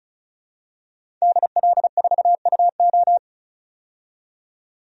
Morse code, heard as a keyed tone, about 36 wpm, 700 Hz. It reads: DL4UO
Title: DXCC-100 Call Signs - 5 Characters - 35wpm